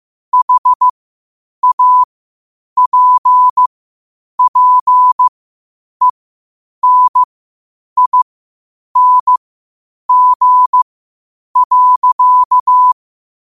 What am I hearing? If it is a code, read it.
Morse code, heard as a keyed tone, about 15 wpm, spaced as for 8 wpm, 1000 Hz. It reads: HAPPENING.
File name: qotdfiles / 8wpm